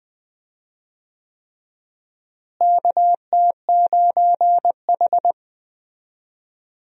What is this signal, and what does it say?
Morse code, heard as a keyed tone, about 21 wpm, 700 Hz. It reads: KT9H
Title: USA Call Signs - 4 Characters - Repeated 3x - 20wpm